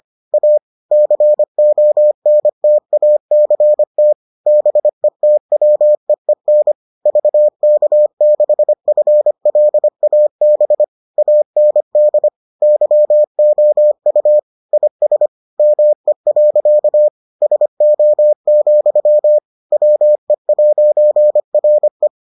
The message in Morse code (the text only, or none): A CONTACT BETWEEN VK6FLAB AND YOU IS ME. SO, WE'RE